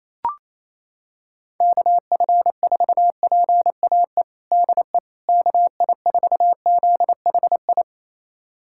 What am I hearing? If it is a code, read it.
Morse code, heard as a keyed tone, about 28 wpm, 700 Hz. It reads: KF4PAE DE KI4ZHI
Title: USA Callsign DE Callsign - 6 Characters - 28wpm